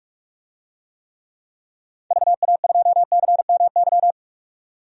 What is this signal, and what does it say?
Morse code, heard as a keyed tone, about 46 wpm, 700 Hz. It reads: VA2CMY
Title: DXCC-100 Call Signs - 6 Characters - 45wpm